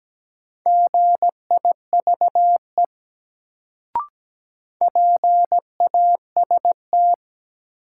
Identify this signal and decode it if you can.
Morse code, heard as a keyed tone, about 17 wpm, 700 Hz. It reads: GIVE PAST
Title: Sets of 2 Words - New Word Drill - 17wpm